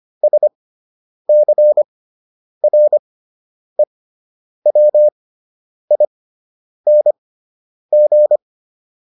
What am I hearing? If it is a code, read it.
Morse code, heard as a keyed tone, about 25 wpm, spaced as for 9 wpm, 600 Hz. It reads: SCREWING
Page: fundamental-practice-files